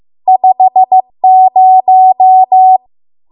Morse code, heard as a keyed tone, about 15 wpm, 750 Hz. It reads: 50